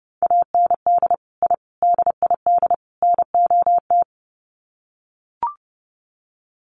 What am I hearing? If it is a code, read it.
Morse code, heard as a keyed tone, about 30 wpm, 700 Hz. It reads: AND I DID NOT
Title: Sets of 4 Words 30wpm